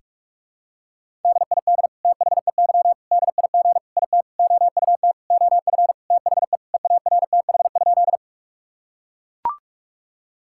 Morse code, heard as a keyed tone, about 41 wpm, 700 Hz. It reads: DID THEY DIG IT OUT OF THE EARTH?
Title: Sentences from Top 1600 Words 45wpm